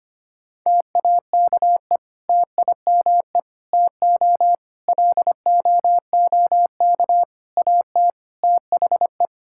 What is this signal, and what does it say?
Morse code, heard as a keyed tone, about 25 wpm, 700 Hz. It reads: TAKE TIME TO LOOK AT THE
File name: practice / 25wpm / Sentences with 3-Word Phrases - 25wpm